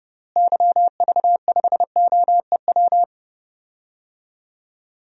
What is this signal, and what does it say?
Morse code, heard as a keyed tone, about 30 wpm, 700 Hz. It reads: YV5OEW
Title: DXCC-100 Call Signs - 6 Characters - Repeated 2x - 30wpm